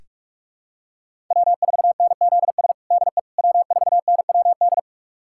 Morse code, heard as a keyed tone, about 41 wpm, 700 Hz. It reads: W4NZS DE W4NWD
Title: USA Callsign DE Callsign - 5 Characters - 45wpm